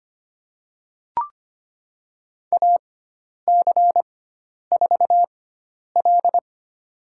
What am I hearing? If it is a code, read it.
Morse code, heard as a keyed tone, about 25 wpm, spaced as for 10 wpm, 700 Hz. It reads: AC4L
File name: Lesson 16g - Letter c - Call Signs 25wpm